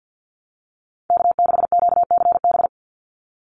Morse code, heard as a keyed tone, about 50 wpm, 700 Hz. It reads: K6QCB
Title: NA SOTA Chaser Callsigns 50wpm